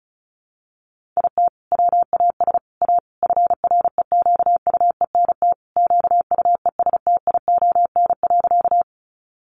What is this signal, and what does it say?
Morse code, heard as a keyed tone, about 34 wpm, 700 Hz. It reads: IT WAS A FREQUENT QUESTION.
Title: Sentences from Top 1400 Words 35wpm